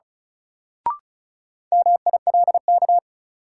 Morse code, heard as a keyed tone, about 36 wpm, 700 Hz. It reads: MILK
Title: Sentences from Top 1500 Words - New Word Drill Alt - 35wpm